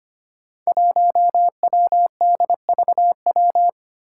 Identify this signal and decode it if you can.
Morse code, heard as a keyed tone, about 25 wpm, 700 Hz. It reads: 1WDVW
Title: Five Challenging Letter-Number Combinations 25wpm